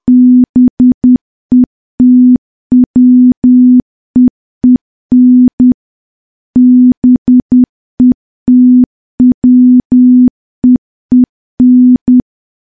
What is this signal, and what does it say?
Morse code, heard as a keyed tone, about 10 wpm, 250 Hz. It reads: BETWEEN BETWEEN
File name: morse / morse-training-10WPM